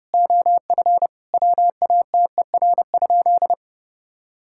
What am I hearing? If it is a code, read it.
Morse code, heard as a keyed tone, about 30 wpm, 700 Hz. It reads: OF WATER?
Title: Sentences from Top 1000 Words - Review Part 5 - 30wpm